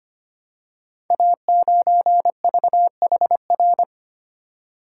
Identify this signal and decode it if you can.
Morse code, heard as a keyed tone, about 25 wpm, 700 Hz. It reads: A9VHR
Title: Code Groups - 5 Letter:Numbers - 25wpm